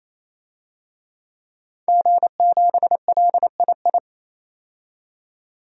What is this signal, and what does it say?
Morse code, heard as a keyed tone, about 28 wpm, 700 Hz. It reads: G7LII
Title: DXCC-100 Call Signs - 5 Characters - 28wpm